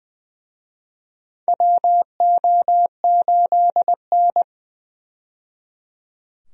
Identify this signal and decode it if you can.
Morse code, heard as a keyed tone, about 21 wpm, 700 Hz. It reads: WO8N
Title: USA Call Signs - 4 Characters - 20wpm